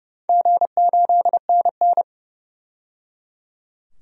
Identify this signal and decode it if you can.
Morse code, heard as a keyed tone, about 30 wpm, 700 Hz. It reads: G8NN
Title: DXCC-100 Call Signs - 4 Characters - 30wpm